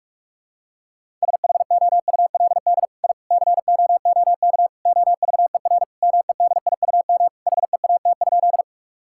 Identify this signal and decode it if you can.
Morse code, heard as a keyed tone, about 46 wpm, 700 Hz. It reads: SHOULD I COOK OVER MEDIUM HEAT?
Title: Sentences from Top 1000 Words - Review Part 6 - 45wpm